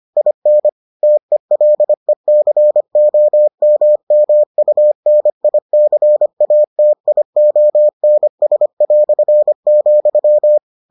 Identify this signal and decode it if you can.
Morse code, heard as a keyed tone, about 25 wpm, 600 Hz. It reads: IN TELECOMMUNICATIONS",